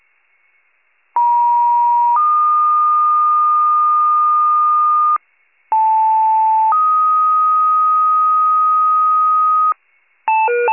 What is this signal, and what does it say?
Two-tone page, heard as a one-second tone, then a three-second tone, about 950 then 1220 Hz, about 880 then 1250 Hz. It is heard from 1.2 to 9.7 s.
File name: pager / 94 RESCUE OFC / wcalert